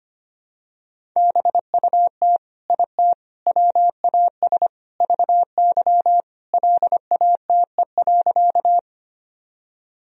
Morse code, heard as a keyed tone, about 25 wpm, 700 Hz. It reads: BUT IT WAS VY LATE.
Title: Sentences from Top 300 Words and Abbreviations 25wpm